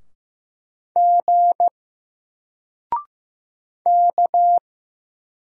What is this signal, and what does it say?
Morse code, heard as a keyed tone, about 15 wpm, 700 Hz. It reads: G K
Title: Single Letter-Number Alt 15wpm